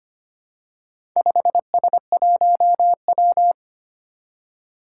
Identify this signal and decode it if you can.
Morse code, heard as a keyed tone, about 25 wpm, 700 Hz. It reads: 5S1W
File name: practice / 25wpm / Four Letter-Number Combinations - 25wpm